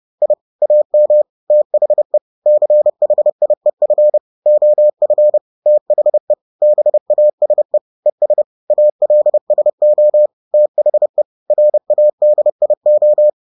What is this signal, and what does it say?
Morse code, heard as a keyed tone, about 30 wpm, 600 Hz. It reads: I AM THE CHIEF OF THE BASE ES ALSO THE RADIO